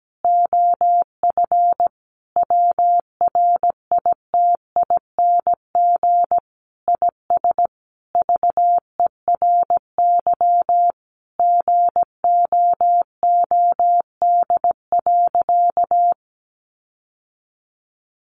Morse code, heard as a keyed tone, about 17 wpm, 700 Hz. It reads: OF WRITING IS VERY GOOD.
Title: Sentences from Top 800 Words 17wpm